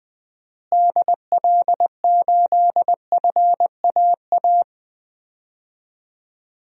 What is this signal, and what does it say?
Morse code, heard as a keyed tone, about 20 wpm, 700 Hz. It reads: DL8FAA
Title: DXCC-100 Call Signs - 6 Characters - Repeated 2x - 20wpm